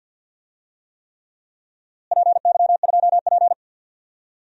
Morse code, heard as a keyed tone, about 53 wpm, 700 Hz. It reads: PY2P
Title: DXCC-100 Call Signs - 4 Characters - Repeated 3x - 50wpm